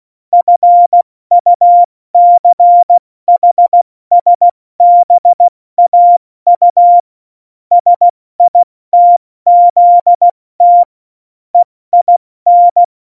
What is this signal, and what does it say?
Morse code, heard as a keyed tone, about 16 wpm, 700 Hz. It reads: FUCHSBAU SITZT EIN
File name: Morsekurs / Witze / Witz 28. August 2018, 70 F80